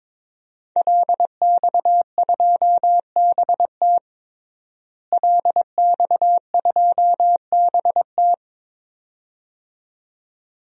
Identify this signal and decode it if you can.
Morse code, heard as a keyed tone, about 21 wpm, 700 Hz. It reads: LX2BT LX2BT
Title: DXCC-100 Call Signs - 5 Characters - Repeated 2x - 22wpm